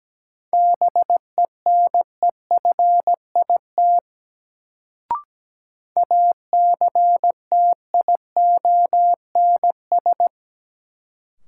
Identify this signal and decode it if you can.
Morse code, heard as a keyed tone, about 17 wpm, 700 Hz. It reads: BENEFIT ACTIONS
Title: Sentences from Top 1000 Words - Review Part 6 - New Word Drill - 17wpm